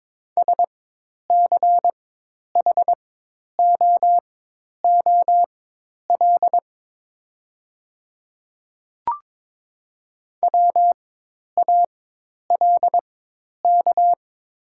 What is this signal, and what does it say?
Morse code, heard as a keyed tone, about 22 wpm, spaced as for 10 wpm, 700 Hz. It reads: SCHOOL WALK
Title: 4x Character Spacing - Top 500 Words - 22wpm